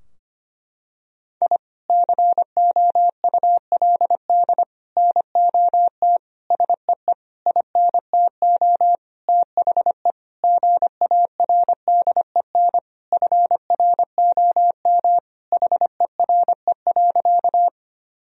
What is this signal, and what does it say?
Morse code, heard as a keyed tone, about 25 wpm, 700 Hz. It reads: I COULD NOT SEE INTO THE GARDEN FROM HERE.
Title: Sentences from Top 1000 Words - Review Part 4 - 25wpm